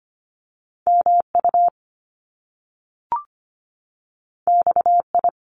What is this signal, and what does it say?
Morse code, heard as a keyed tone, about 25 wpm, 700 Hz. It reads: MU XI
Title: Top Two Letter Combinations in Words 25wpm